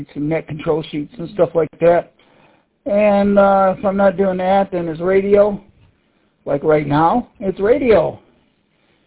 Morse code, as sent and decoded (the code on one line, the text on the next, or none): none